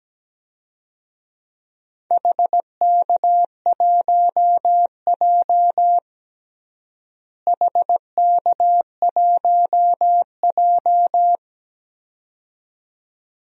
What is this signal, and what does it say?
Morse code, heard as a keyed tone, about 17 wpm, 700 Hz. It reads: HK1J HK1J
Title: DXCC-100 Call Signs - 4 Characters - Repeated 2x - 17wpm